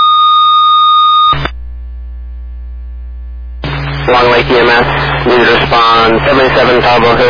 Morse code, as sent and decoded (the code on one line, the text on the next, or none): none